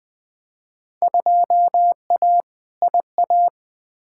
Morse code, heard as a keyed tone, about 21 wpm, 700 Hz. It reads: 2A IA
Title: Field Day Exchanges - 20wpm